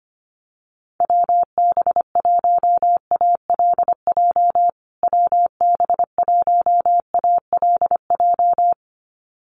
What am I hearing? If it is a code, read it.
Morse code, heard as a keyed tone, about 25 wpm, 700 Hz. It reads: WB1ALJ WB1ALJ
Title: USA Call Signs Repeated 2x 25wpm